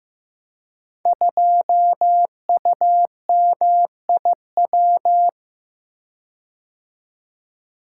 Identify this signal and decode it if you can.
Morse code, heard as a keyed tone, about 15 wpm, 700 Hz. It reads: 2UMIW
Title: Code Groups - 5 Letter:Numbers - 15wpm